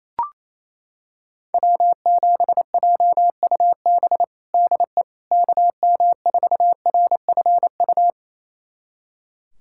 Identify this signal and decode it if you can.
Morse code, heard as a keyed tone, about 28 wpm, 700 Hz. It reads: W7JUB DE KM4RFU
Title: USA Callsign DE Callsign 28wpm